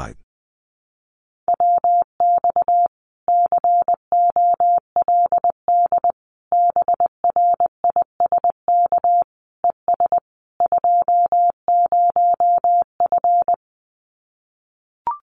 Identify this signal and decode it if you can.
Morse code, heard as a keyed tone, about 20 wpm, 700 Hz. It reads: WX COLD BRISK ES 20F
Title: Common QSO Phrases Encore 20wpm